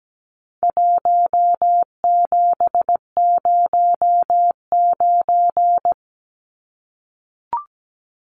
Morse code, heard as a keyed tone, about 17 wpm, 700 Hz. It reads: 1709
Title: Numbers - Contest Style - 4-Digit 17wpm